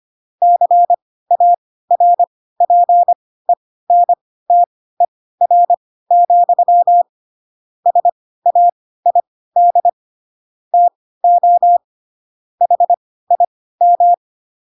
Morse code, heard as a keyed tone, about 25 wpm, spaced as for 16 wpm, 700 Hz. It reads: CARPENTER, SAID TO HIM
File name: Pinocchio / Pinocchio ch2 10 25 ik1hge